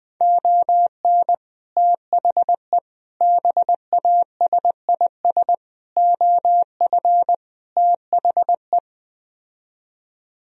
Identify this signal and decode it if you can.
Morse code, heard as a keyed tone, about 21 wpm, 700 Hz. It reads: ON THE BASIS OF THE